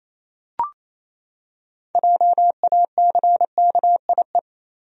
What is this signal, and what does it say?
Morse code, heard as a keyed tone, about 28 wpm, 700 Hz. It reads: JACKIE
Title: USA Common Names - 28wpm